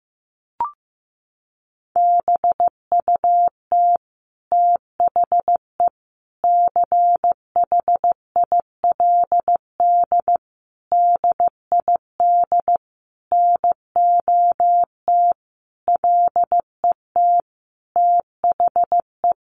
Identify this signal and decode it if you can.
Morse code, heard as a keyed tone, about 15 wpm, 700 Hz. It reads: BUT THE CHILD DID NOT LET THE
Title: Sentences from Top 1400 Words 15wpm